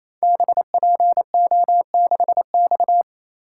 Morse code, heard as a keyed tone, about 28 wpm, 700 Hz. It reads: BPO6X
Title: Code Groups - 5 Letter:Numbers - 28wpm